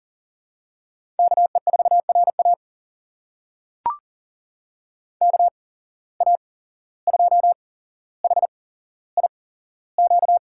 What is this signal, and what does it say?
Morse code, heard as a keyed tone, about 40 wpm, spaced as for 16 wpm, 700 Hz. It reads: KE4RA KA2HIQ
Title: USA Call Signs - 8x Character Spacing - 40wpm